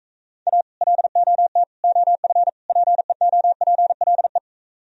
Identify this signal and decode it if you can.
Morse code, heard as a keyed tone, about 44 wpm, 700 Hz. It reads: A LOT OF PEOPLE